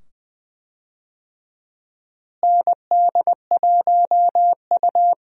Morse code, heard as a keyed tone, about 20 wpm, 700 Hz. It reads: ND1U
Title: USA Call Signs - 4 Characters - Repeated 2x - 20wpm